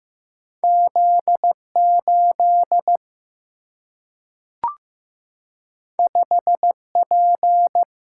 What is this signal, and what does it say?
Morse code, heard as a keyed tone, about 15 wpm, 700 Hz. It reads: Z8 5P
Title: Single-Letter Single-Number 15wpm